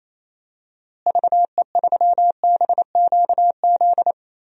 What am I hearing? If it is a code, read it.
Morse code, heard as a keyed tone, about 28 wpm, 700 Hz. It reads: VE3BQZ
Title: DXCC-100 Call Signs - 6 Characters - Repeated 3x - 28wpm